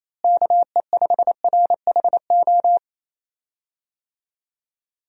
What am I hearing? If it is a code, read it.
Morse code, heard as a keyed tone, about 28 wpm, 700 Hz. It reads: KE5RHO
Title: USA Call Signs - 6 Characters - Repeated 3x - 28wpm